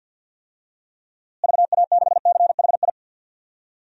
Morse code, heard as a keyed tone, about 47 wpm, 700 Hz. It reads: VA6CHI
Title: DXCC-100 Call Signs - 6 Characters - Repeated 2x - 50wpm